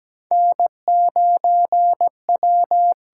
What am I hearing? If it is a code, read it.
Morse code, heard as a keyed tone, about 17 wpm, 700 Hz. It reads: N9W
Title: USA Call Signs - 8x Character Spacing - 17wpm